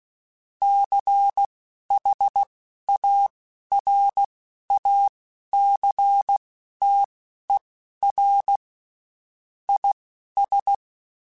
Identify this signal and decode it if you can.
Morse code, heard as a keyed tone, about 16 wpm, spaced as for 11 wpm, 800 Hz. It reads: CHARACTER IS
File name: Quotes Spaced / Quotes Spaced-mp3s / Character sp @15